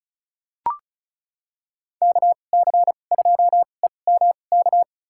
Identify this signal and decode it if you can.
Morse code, heard as a keyed tone, about 34 wpm, spaced as for 25 wpm, 700 Hz. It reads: KC2EMK
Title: USA Call Signs - 6 Characters - 2x Character Spacing - 35wpm